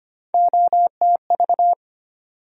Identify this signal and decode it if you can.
Morse code, heard as a keyed tone, about 25 wpm, 700 Hz. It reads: OTV
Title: USA Call Sign Postfixes - 3 Letters 25wpm